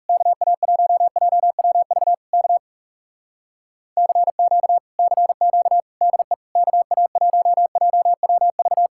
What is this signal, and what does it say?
Morse code, heard as a keyed tone, about 40 wpm, 700 Hz. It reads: KA1JWV K CQ CQ DE KA1JWV